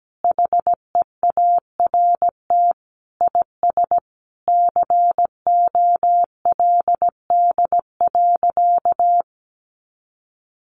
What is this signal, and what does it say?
Morse code, heard as a keyed tone, about 17 wpm, 700 Hz. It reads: HEART IS COLD.
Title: Sentences from Top 1200 Words 17wpm